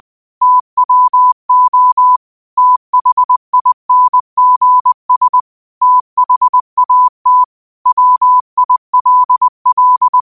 Morse code, heard as a keyed tone, about 20 wpm, 1000 Hz. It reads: TWO THINGS THAT WILL
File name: 20wpm